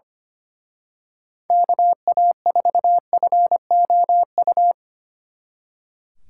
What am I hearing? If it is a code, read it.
Morse code, heard as a keyed tone, about 29 wpm, 700 Hz. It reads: KA4FOU
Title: Speed Racing - USA 2x3 Call Signs 25wpm